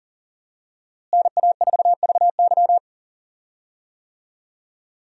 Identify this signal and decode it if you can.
Morse code, heard as a keyed tone, about 40 wpm, 700 Hz. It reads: NA4VY
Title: USA Call Signs - 5 Characters - Repeated 2x - 40wpm